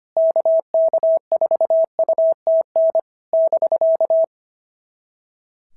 Morse code, heard as a keyed tone, about 25 wpm, 690 Hz. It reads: KK4UTN <BK>
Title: SOTA QSO Practice 25wpm